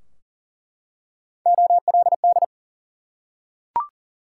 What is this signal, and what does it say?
Morse code, heard as a keyed tone, about 39 wpm, 700 Hz. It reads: OLD
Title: Three Letter Words - 40wpm